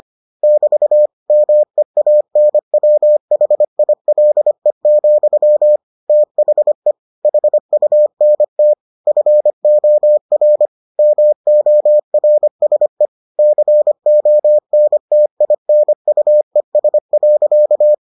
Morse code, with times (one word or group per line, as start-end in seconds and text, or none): 0.4-1.1 <BT>
1.3-5.8 MEANWHILE,
6.1-6.9 THE
7.2-8.7 HUNT
9.1-10.7 FOR
11.0-13.1 MORSE
13.4-17.9 CONTINUES.